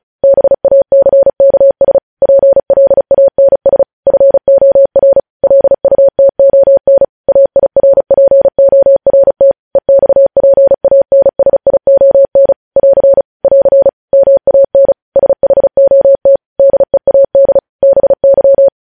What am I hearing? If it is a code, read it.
Morse code, heard as a keyed tone, about 36 wpm, 550 Hz. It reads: BACKS PLANS FOR LUTON AIRPORT EXPANSION <AR> <AR> MAN SHOT DEAD BY
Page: news headlines in morse code at 35 wpm